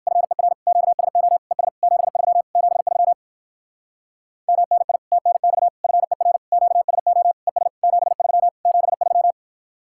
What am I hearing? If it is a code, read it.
Morse code, heard as a keyed tone, about 59 wpm, 700 Hz. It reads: FER QSO ES 73 73 MNI TNX FER QSO ES 73 73